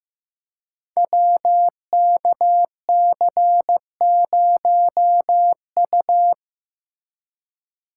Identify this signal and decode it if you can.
Morse code, heard as a keyed tone, about 15 wpm, 700 Hz. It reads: WKC0U
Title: Code Groups - 5 Letter:Numbers - 15wpm